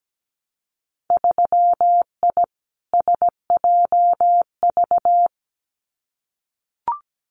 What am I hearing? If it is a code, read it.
Morse code, heard as a keyed tone, about 17 wpm, 700 Hz. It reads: 3I SJV
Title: Winter Field Day Exchanges Encore 17wpm